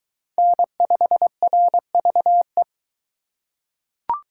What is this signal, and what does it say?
Morse code, heard as a keyed tone, about 20 wpm, 700 Hz. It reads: N5RVE